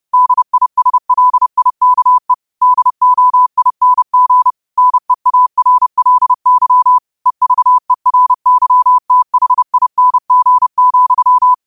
Morse code, heard as a keyed tone, about 30 wpm, 1000 Hz. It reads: DISLIKE DOING NEARLY EVERYTHING,
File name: qotdfiles / 30wpm